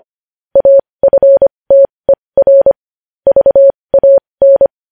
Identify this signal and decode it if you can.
Morse code, heard as a keyed tone, about 25 wpm, spaced as for 20 wpm, 550 Hz. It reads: AFTER VAN
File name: podcast / News headlines-1x-25x20